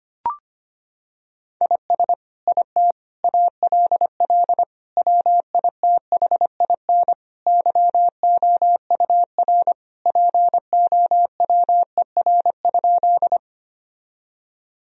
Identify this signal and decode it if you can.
Morse code, heard as a keyed tone, about 25 wpm, 700 Hz. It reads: IS IT ALL WITHIN YOUR POWER?
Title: Sentences from Top 1000 Words - Review Part 2 - 25wpm